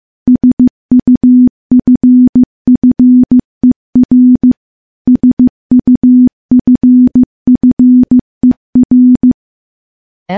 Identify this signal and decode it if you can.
Morse code, heard as a keyed tone, about 15 wpm, 250 Hz. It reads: SUFFER SUFFER
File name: morse-training-15WPM